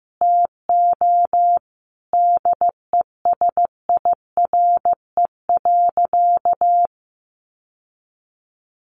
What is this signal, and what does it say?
Morse code, heard as a keyed tone, about 15 wpm, 700 Hz. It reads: TO DESIRE.
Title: Sentences from Top 1100 Words 15wpm